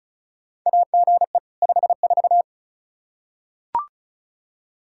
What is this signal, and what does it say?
Morse code, heard as a keyed tone, about 34 wpm, 700 Hz. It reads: AGE 54
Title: Practice QSO - Exchange 3 - More Info - 35wpm